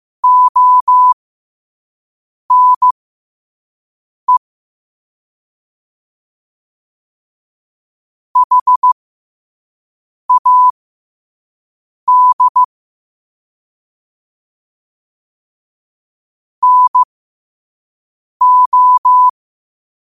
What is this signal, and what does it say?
Morse code, heard as a keyed tone, about 15 wpm, spaced as for 5 wpm, 1000 Hz. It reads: ONE HAD NO
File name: qotdfiles / 5wpm